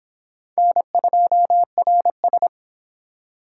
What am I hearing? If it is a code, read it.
Morse code, heard as a keyed tone, about 26 wpm, 700 Hz. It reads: N2RS